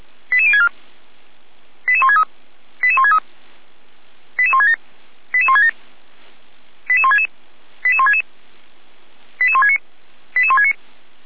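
Five-tone selective call (ZVEI), heard as quick runs of five tones, tones of about 70 ms. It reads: 88064 88163 88163 88167 88167 88166 88166 88169 88169
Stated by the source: ZVEI five-tone selective call received on an SDR.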